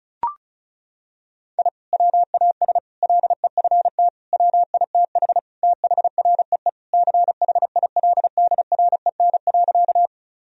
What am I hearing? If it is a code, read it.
Morse code, heard as a keyed tone, about 36 wpm, 700 Hz. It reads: I WAS LEFT WITH THREE CHILDREN.
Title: Sentences from Top 1000 Words - Review Part 1 - 35wpm